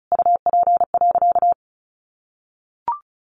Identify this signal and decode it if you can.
Morse code, heard as a keyed tone, about 34 wpm, 700 Hz. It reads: UP.
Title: Sentences from Top 1500 Words 35wpm